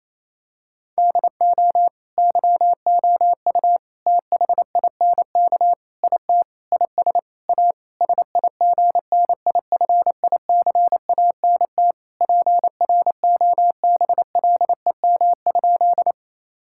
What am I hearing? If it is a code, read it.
Morse code, heard as a keyed tone, about 28 wpm, 700 Hz. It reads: DO YOU THINK IT IS A SIGNIFICANT PROBLEM?
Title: Sentences from Top 1000 Words - Review Part 6 - 28wpm